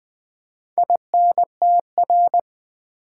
Morse code, heard as a keyed tone, about 20 wpm, 700 Hz. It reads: INTR
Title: Build A Word - 4 Letters At A Time - 10-Letter Words - 20wpm